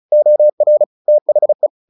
Morse code, heard as a keyed tone, about 34 wpm, 600 Hz. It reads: OR THE
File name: the old man and the sea 2 35wpm